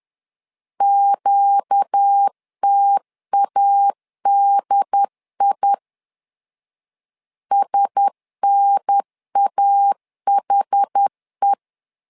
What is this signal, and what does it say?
Morse code, heard as a keyed tone, about 10 wpm, 800 Hz. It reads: QTADI SNAHE
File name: LEZIONI / 5 LBVC / 5 LBV 10wpm 2248